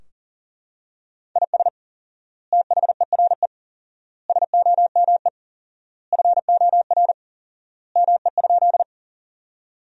Morse code, heard as a keyed tone, about 40 wpm, 700 Hz. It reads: IS THERE SOME FOR ME?
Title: 4x Word-Spacing - Sentences from Top 200 Words - 40wpm